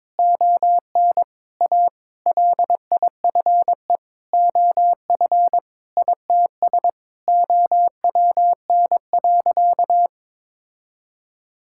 Morse code, heard as a keyed tone, about 22 wpm, 700 Hz. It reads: ON A LIFE OF ITS OWN.
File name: practice / 22wpm / Sentences with 8-Word Phrases - 22wpm